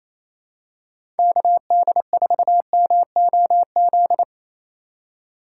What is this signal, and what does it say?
Morse code, heard as a keyed tone, about 28 wpm, 700 Hz. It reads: KD4MOZ
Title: USA Call Signs - 6 Characters - 2x Character Spacing - 28wpm